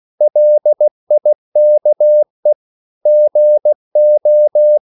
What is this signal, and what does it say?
Morse code, heard as a keyed tone, about 16 wpm, 600 Hz. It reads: LIKE GO
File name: CW / Practice / 500MostCommonEnglishWords16